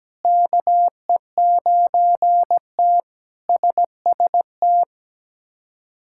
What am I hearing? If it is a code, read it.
Morse code, heard as a keyed tone, about 17 wpm, 700 Hz. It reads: KE9T SST
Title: K1USN SST QSO Practice - 17wpm